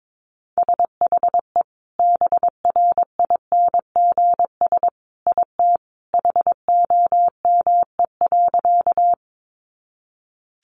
Morse code, heard as a keyed tone, about 22 wpm, 700 Hz. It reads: SHE BRINGS IT HOME.
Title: Easy Phrases 22wpm